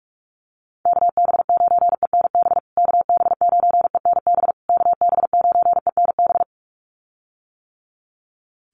Morse code, heard as a keyed tone, about 44 wpm, 700 Hz. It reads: KB9ENB KB9ENB KB9ENB
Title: USA Call Signs Repeated 3x 45wpm